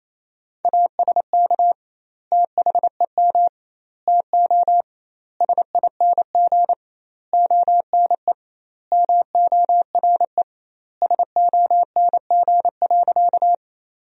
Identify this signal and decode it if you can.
Morse code, heard as a keyed tone, about 28 wpm, 700 Hz. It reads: ASK THEM TO SING ONE MORE SONG.
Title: 2x Word-Spacing - Sentences from Top 400 Words - 28wpm